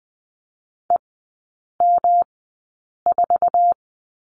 Morse code, heard as a keyed tone, about 20 wpm, spaced as for 8 wpm, 700 Hz. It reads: EM4
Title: 138 Lesson 23e - Letter f - Three Characters 20wpm